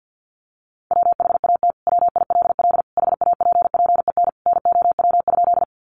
Morse code, heard as a keyed tone, about 50 wpm, 700 Hz. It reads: WHAT WILL HAPPEN NOW?
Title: Sentences from Top 400 Words 50wpm